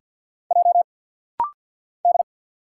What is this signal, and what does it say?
Morse code, heard as a keyed tone, about 50 wpm, 700 Hz. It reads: J D
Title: Single Letter-Number Alt - Rapid-Fire 50wpm